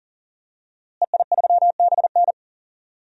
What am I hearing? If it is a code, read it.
Morse code, heard as a keyed tone, about 40 wpm, 700 Hz. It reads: EI3BN
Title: DXCC-100 Call Signs - 5 Characters - Repeated 3x - 40wpm